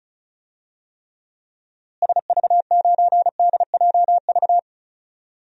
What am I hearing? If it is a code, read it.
Morse code, heard as a keyed tone, about 34 wpm, 700 Hz. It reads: SV9DJV
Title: DXCC-100 Call Signs - 6 Characters - Repeated 2x - 35wpm